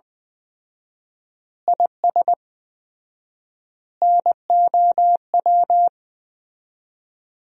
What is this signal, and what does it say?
Morse code, heard as a keyed tone, about 20 wpm, 700 Hz. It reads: IS NOW
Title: 4x Word-Spacing - Sentences from Top 400 Words - 20wpm